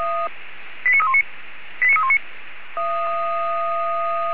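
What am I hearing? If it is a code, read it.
Five-tone selective call (ZVEI), heard as quick runs of five tones, tones of about 70 ms. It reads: 89319 89319
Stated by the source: Five-tone ZVEI selective call received on SDR by Feuerwehr Orlamünde.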